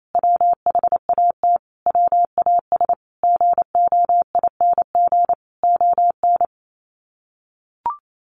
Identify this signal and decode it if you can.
Morse code, heard as a keyed tone, about 28 wpm, 700 Hz. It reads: WHAT WAS GOING ON
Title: Sets of 5 Words Encore2 28wpm